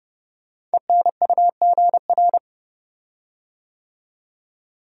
Morse code, heard as a keyed tone, about 30 wpm, 700 Hz. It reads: ENUGR
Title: Code Groups - 5 Letters - 30wpm